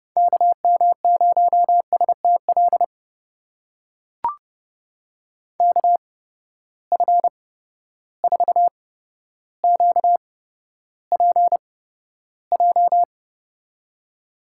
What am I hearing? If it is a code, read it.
Morse code, heard as a keyed tone, about 29 wpm, spaced as for 13 wpm, 700 Hz. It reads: KM0STL KF4QPJ
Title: USA Call Signs - 6 Characters - 8x Character Spacing - 30wpm